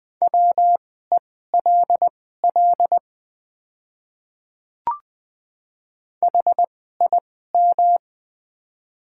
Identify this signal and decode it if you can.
Morse code, heard as a keyed tone, about 20 wpm, spaced as for 14 wpm, 700 Hz. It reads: WELL HIM
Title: 2x Character Spacing - Top 200 Words - 20wpm